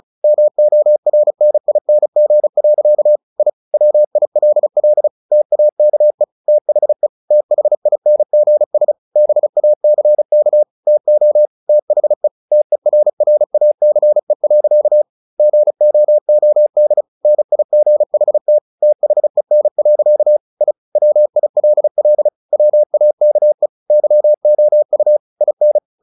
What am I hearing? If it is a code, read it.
Morse code, heard as a keyed tone, about 34 wpm, 600 Hz. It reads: MORNING. I WILL TAKE THE THINGS BACK TO THE TERRACE. GOOD NIGHT THEN. I WILL WAKE YOU IN
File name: the old man and the sea / the old man and the sea 1 35wpm